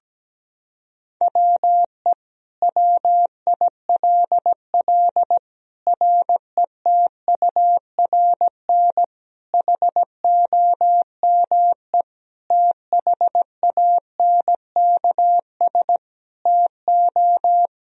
Morse code, heard as a keyed tone, about 17 wpm, 700 Hz. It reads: WE WILL RETURN HOME THANKS TO
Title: Sentences from Top 1000 Words - Review Part 3 - 17wpm